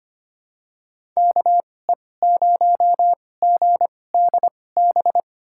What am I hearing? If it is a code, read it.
Morse code, heard as a keyed tone, about 25 wpm, spaced as for 18 wpm, 700 Hz. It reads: KE0GDB
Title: USA Call Signs - 6 Characters - 2x Character Spacing - 25wpm